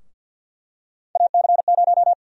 Morse code, heard as a keyed tone, about 50 wpm, 700 Hz. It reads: AC0
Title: USA Call Sign Prefixes - 50wpm